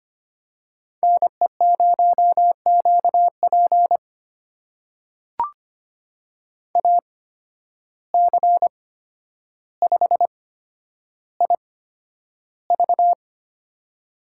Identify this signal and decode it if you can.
Morse code, heard as a keyed tone, about 25 wpm, spaced as for 11 wpm, 700 Hz. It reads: NE0QP AC5IV
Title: USA Call Signs - 8x Character Spacing - 25wpm